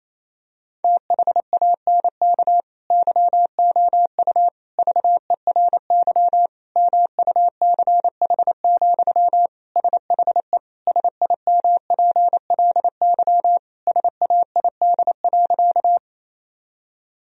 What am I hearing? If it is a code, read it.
Morse code, heard as a keyed tone, about 28 wpm, 700 Hz. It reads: THANK YOU VERY MUCH, SHE SIMPLY SAID.
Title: Sentences from Top 1000 Words - Review Part 4 - 28wpm